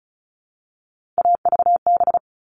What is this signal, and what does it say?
Morse code, heard as a keyed tone, about 34 wpm, 700 Hz. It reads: AVB
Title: All Cut Numbers - 3-Digit 35wpm